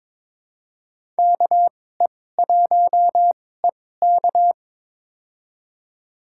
Morse code, heard as a keyed tone, about 22 wpm, spaced as for 16 wpm, 700 Hz. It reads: KE1EK
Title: USA Call Signs - 5 Characters - 2x Character Spacing - 22wpm